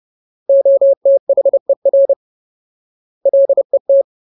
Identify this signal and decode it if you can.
Morse code, heard as a keyed tone, about 30 wpm, 550 Hz. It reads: OTHER LET
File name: Ultimate Word List / 30WPM Ultimate